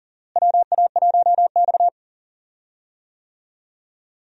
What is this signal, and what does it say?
Morse code, heard as a keyed tone, about 40 wpm, 700 Hz. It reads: WA1X
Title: USA Call Signs - 4 Characters - Repeated 2x - 40wpm